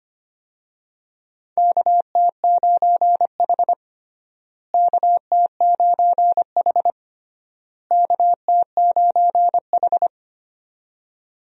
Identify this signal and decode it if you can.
Morse code, heard as a keyed tone, about 25 wpm, 700 Hz. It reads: KT9H KT9H KT9H
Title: USA Call Signs - 4 Characters - Repeated 3x - 25wpm